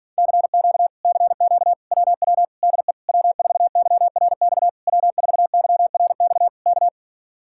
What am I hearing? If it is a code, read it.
Morse code, heard as a keyed tone, about 52 wpm, 700 Hz. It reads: CQ CQ WW DE W4YRX W4YRX K